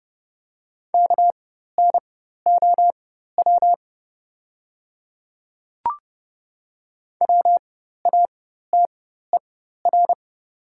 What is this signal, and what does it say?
Morse code, heard as a keyed tone, about 30 wpm, spaced as for 14 wpm, 700 Hz. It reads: KNOW WATER
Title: 4x Character Spacing - Top 100 Words - 30wpm